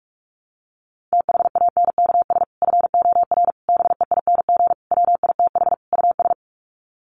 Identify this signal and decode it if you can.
Morse code, heard as a keyed tone, about 44 wpm, 700 Hz. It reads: THANKS FOR BEING WITH US
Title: Sets of 5 Words 45wpm